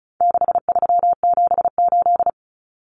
Morse code, heard as a keyed tone, about 34 wpm, 700 Hz. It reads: 6378
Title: Numbers - 4-Digit 35wpm